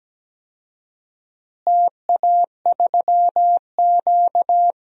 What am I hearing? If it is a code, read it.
Morse code, heard as a keyed tone, about 17 wpm, 700 Hz. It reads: TA3Q
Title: DXCC-100 Call Signs - 4 Characters - Repeated 3x - 17wpm